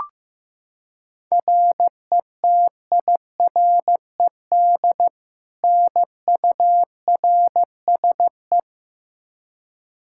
Morse code, heard as a keyed tone, about 15 wpm, 700 Hz. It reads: RETIRED NURSE
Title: K1USN SST QSO Practice 15wpm